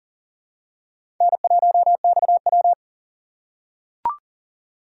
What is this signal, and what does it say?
Morse code, heard as a keyed tone, about 40 wpm, 700 Hz. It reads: N1XW
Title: USA Call Signs - 4 Characters - 6x Character Spacing - 40wpm